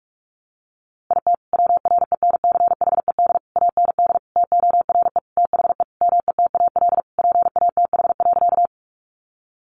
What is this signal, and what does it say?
Morse code, heard as a keyed tone, about 44 wpm, 700 Hz. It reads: IT WRENCHED AND TORE THE METAL PATH.
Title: A Sound Of Thunder 45wpm